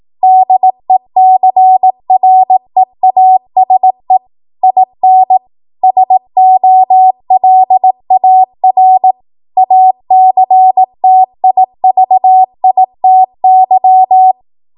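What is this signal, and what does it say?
Morse code, heard as a keyed tone, about 18 wpm, 750 Hz. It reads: DECREASE IN SOLAR ACTIVITY
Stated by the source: Code practice at 18 WPM, a transition run bridging 15 to 20 WPM.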